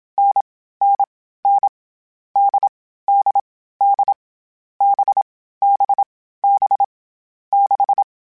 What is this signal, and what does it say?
Morse code, heard as a keyed tone, about 27 wpm, 800 Hz. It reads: N N N D D D B B B 6
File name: Short Runs / Short Runs-mp3s / NDB NDB6 @25